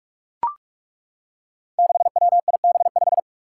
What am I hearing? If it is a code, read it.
Morse code, heard as a keyed tone, about 42 wpm, 700 Hz. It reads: 6WIB5
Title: Five Challenging Letter-Number Combinations 45wpm